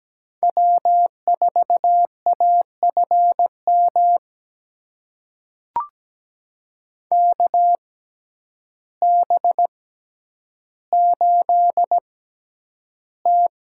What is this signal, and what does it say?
Morse code, heard as a keyed tone, about 17 wpm, spaced as for 9 wpm, 700 Hz. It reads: W4AFM KB8T
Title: USA Call Signs - 6x Character Spacing - 17wpm